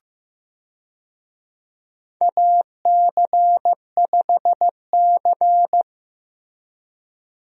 Morse code, heard as a keyed tone, about 16 wpm, 700 Hz. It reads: AC5C
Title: USA Call Signs - 4 Characters - Repeated 2x - 15wpm